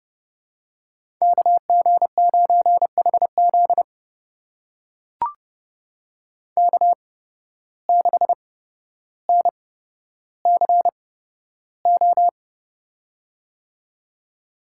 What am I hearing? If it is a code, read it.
Morse code, heard as a keyed tone, about 30 wpm, spaced as for 13 wpm, 700 Hz. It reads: KG9HZ K6NCO
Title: USA Call Signs - 5 Characters - 8x Character Spacing - 30wpm